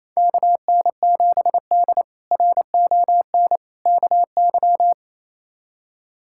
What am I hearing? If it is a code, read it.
Morse code, heard as a keyed tone, about 28 wpm, 700 Hz. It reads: KN7D RON KY
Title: K1USN SST QSO Practice - 28wpm